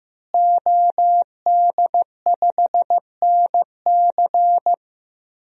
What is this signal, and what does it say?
Morse code, heard as a keyed tone, about 15 wpm, 700 Hz. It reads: OD5NC
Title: DXCC-100 Call Signs - 5 Characters - Repeated 2x - 15wpm